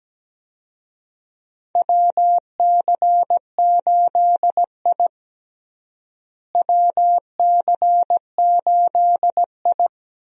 Morse code, heard as a keyed tone, about 17 wpm, 700 Hz. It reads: WC8I WC8I
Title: USA Call Signs - 4 Characters - Repeated 2x - 17wpm